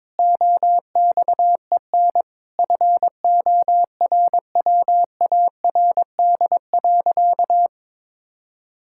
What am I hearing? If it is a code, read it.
Morse code, heard as a keyed tone, about 22 wpm, 700 Hz. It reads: OXEN FORWARD.
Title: Pangram Challenge v2 - 22wpm